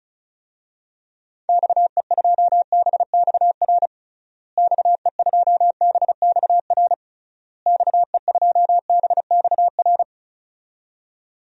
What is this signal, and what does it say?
Morse code, heard as a keyed tone, about 34 wpm, 700 Hz. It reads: XE2BXR XE2BXR XE2BXR
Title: DXCC-100 Call Signs - 6 Characters - Repeated 3x - 35wpm